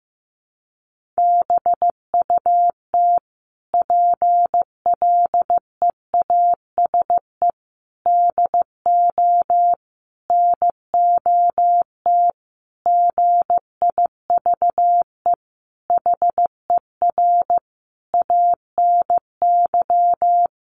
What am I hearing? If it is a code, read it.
Morse code, heard as a keyed tone, about 15 wpm, 700 Hz. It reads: BUT PLEASE DO NOT GIVE HER ANY
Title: Sentences from Top 1500 Words 15wpm